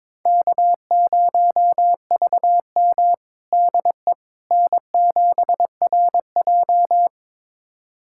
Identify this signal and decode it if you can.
Morse code, heard as a keyed tone, about 22 wpm, 700 Hz. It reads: K0VM DE N7RJ
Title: USA Callsign DE Callsign - 4 Characters - 22wpm